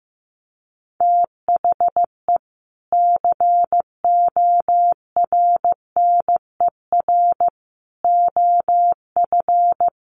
Morse code, heard as a keyed tone, about 15 wpm, 700 Hz. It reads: THE CORNER OF
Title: Sets of 5 Words Encore2 15wpm